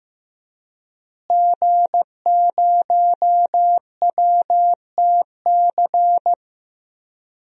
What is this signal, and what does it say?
Morse code, heard as a keyed tone, about 15 wpm, 700 Hz. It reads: G0WTC
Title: DXCC-100 Call Signs - 5 Characters - Repeated 2x - 15wpm